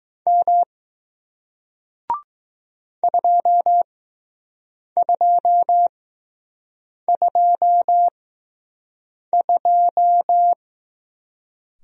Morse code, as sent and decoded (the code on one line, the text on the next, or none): -- / ..--- ..--- ..--- ..---
M 2222